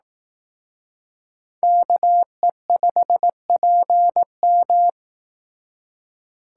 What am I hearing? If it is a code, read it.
Morse code, heard as a keyed tone, about 18 wpm, 700 Hz. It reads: KE5PM
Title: Speed Racing - USA 2x2 Call Signs 15wpm